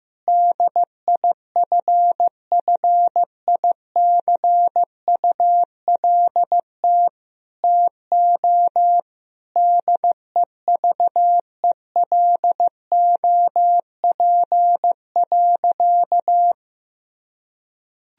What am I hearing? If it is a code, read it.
Morse code, heard as a keyed tone, about 15 wpm, 700 Hz. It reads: DIFFICULT TO DEVELOP.
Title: Sentences from Top 1000 Words - Review Part 2 - 15wpm